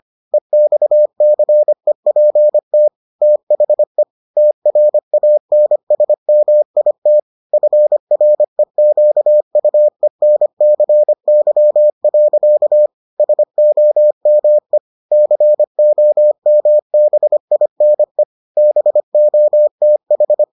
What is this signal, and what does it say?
Morse code, heard as a keyed tone, about 25 wpm, 600 Hz. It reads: EXCEPT THE TRANSMIT FREQUENCY. SOME COMBINE BOTH